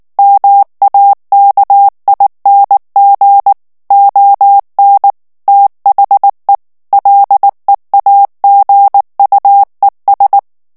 Morse code, heard as a keyed tone, about 19 wpm, 800 Hz. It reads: MAKING ON THE LEAGUES
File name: ANGLAIS texte clair 3 vitesse 18 mots minute